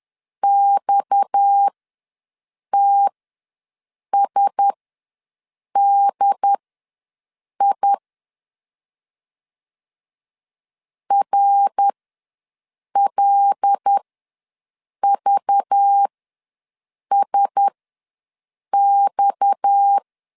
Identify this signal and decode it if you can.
Morse code, heard as a keyed tone, about 10 wpm, spaced as for 6 wpm, 800 Hz. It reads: XTSDI RLVSX
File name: LEZIONI / 6 PKX0 / 6 PKX ws 10wpm 3706